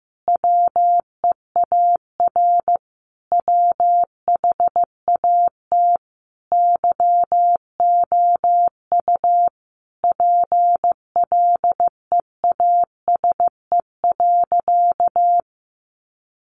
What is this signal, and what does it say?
Morse code, heard as a keyed tone, about 15 wpm, 700 Hz. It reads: WEAR WHAT YOU PLEASE.
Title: Sentences from Top 1500 Words 15wpm